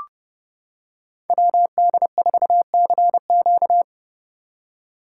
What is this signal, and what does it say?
Morse code, heard as a keyed tone, about 29 wpm, 700 Hz. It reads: WD4CQ